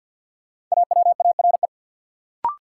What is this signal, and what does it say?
Morse code, heard as a keyed tone, about 51 wpm, 700 Hz. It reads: AWARE
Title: Sentences from Top 1200 Words - New Word Drill Alt - 50wpm